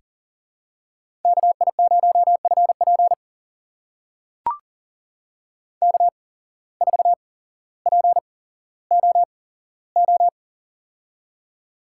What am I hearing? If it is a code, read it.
Morse code, heard as a keyed tone, about 40 wpm, spaced as for 17 wpm, 700 Hz. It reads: KI0FP K4POO
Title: USA Call Signs - 5 Characters - 8x Character Spacing - 40wpm